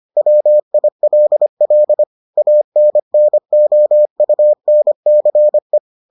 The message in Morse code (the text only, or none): WILL ANNOUNCE